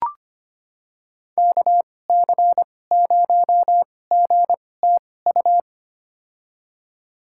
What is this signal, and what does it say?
Morse code, heard as a keyed tone, about 25 wpm, spaced as for 18 wpm, 700 Hz. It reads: KC0GTU
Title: USA Call Signs - 6 Characters - 2x Character Spacing - 25wpm